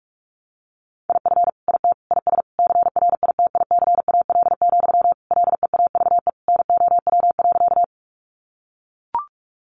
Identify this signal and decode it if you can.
Morse code, heard as a keyed tone, about 44 wpm, 700 Hz. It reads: IF IT IS CRITICAL, LEAVE NOW.
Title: Sentences from Top 1100 Words 45wpm